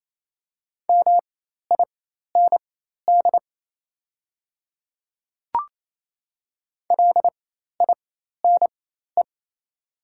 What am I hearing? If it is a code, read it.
Morse code, heard as a keyed tone, about 28 wpm, spaced as for 13 wpm, 700 Hz. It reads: MIND LINE
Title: 4x Character Spacing - Top 500 Words - 28wpm